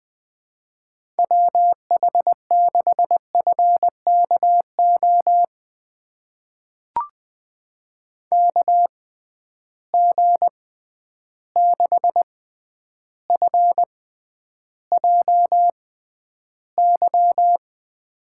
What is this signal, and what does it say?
Morse code, heard as a keyed tone, about 20 wpm, spaced as for 10 wpm, 700 Hz. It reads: WH6FKO KG6FJY
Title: USA Call Signs - 6 Characters - 6x Character Spacing - 20wpm